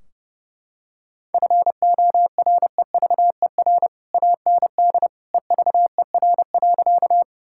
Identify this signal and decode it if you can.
Morse code, heard as a keyed tone, about 30 wpm, 700 Hz. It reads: FOREVER AND EVER.
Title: Sentences from Top 1600 Words 30wpm